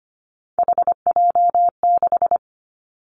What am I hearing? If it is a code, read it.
Morse code, heard as a keyed tone, about 25 wpm, 700 Hz. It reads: HJ6
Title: Callsign Prefixes for Top 100 DXCC - 25wpm